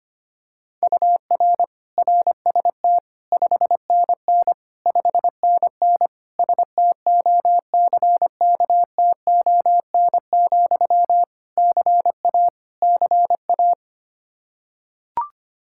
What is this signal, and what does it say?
Morse code, heard as a keyed tone, about 25 wpm, 700 Hz. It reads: UR RST 5NN 5NN STOCKTON, CA CA
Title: Practice QSO - Exchange 1 - Beginning - 25wpm